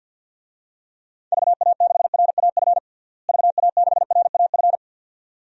Morse code, heard as a keyed tone, about 53 wpm, 700 Hz. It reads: VA6RAF VA6RAF
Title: DXCC-100 Call Signs - 6 Characters - Repeated 2x - 50wpm